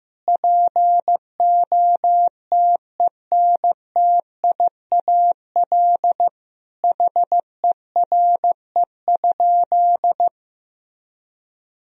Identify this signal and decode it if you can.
Morse code, heard as a keyed tone, about 15 wpm, 700 Hz. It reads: POTENTIAL HERE?
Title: Sentences from Top 1000 Words - Review Part 4 - 15wpm